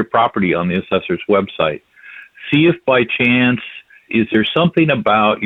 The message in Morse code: none

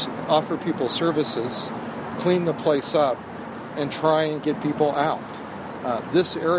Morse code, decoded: none

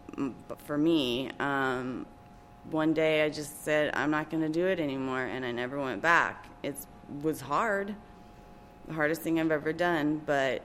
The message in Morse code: none

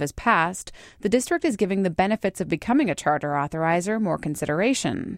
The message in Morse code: none

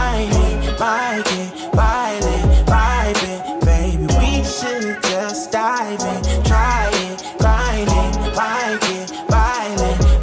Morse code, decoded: none

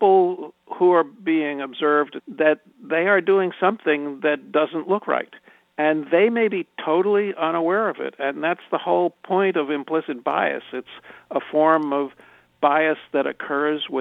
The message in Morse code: none

